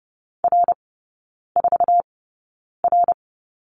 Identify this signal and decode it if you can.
Morse code, heard as a keyed tone, about 30 wpm, spaced as for 9 wpm, 700 Hz. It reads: R4R
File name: 081 Lesson 14e - Number 2 - Three Characters 30wpm